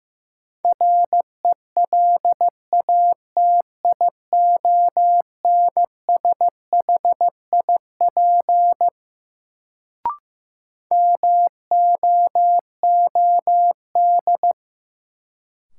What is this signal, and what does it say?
Morse code, heard as a keyed tone, about 15 wpm, 700 Hz. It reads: RELATIONSHIP MOOD
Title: Sentences from Top 1100 Words - New Word Drill Alt - 15wpm